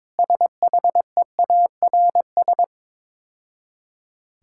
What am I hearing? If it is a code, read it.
Morse code, heard as a keyed tone, about 22 wpm, 690 Hz. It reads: SHEARS
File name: Easy Words - Speed-Racing with Hints - 22wpm